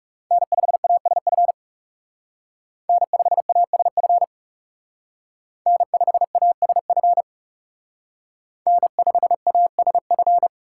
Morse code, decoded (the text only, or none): N5ASF N5ASF N5ASF N5ASF